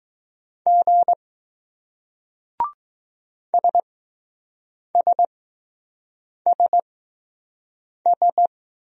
Morse code, heard as a keyed tone, about 20 wpm, spaced as for 6 wpm, 700 Hz. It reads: G SSSS